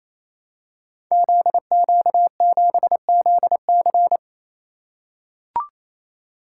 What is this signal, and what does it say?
Morse code, heard as a keyed tone, about 27 wpm, 700 Hz. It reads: ZQ7ZC
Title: Code Groups - 5 Letter:Numbers - 28wpm